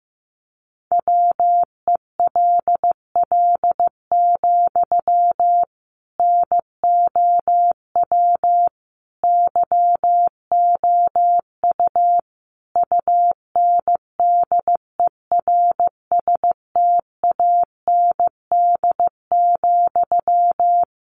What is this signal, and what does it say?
Morse code, heard as a keyed tone, about 15 wpm, 700 Hz. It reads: WELL, NOW YOU UNDERSTAND,
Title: Sentences from Top 600 Words 15wpm